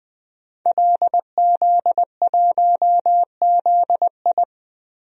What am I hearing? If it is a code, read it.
Morse code, heard as a keyed tone, about 21 wpm, 700 Hz. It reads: LZ1ZI
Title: DXCC-100 Call Signs - 5 Characters - 20wpm